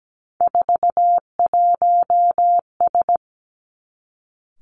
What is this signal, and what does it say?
Morse code, heard as a keyed tone, about 17 wpm, 700 Hz. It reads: 41S
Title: Three Challenging Letter-Number Combinations 17wpm